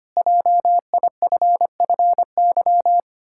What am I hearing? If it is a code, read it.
Morse code, heard as a keyed tone, about 25 wpm, 700 Hz. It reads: JIFFY
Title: Pangram Challenge v2 - 25wpm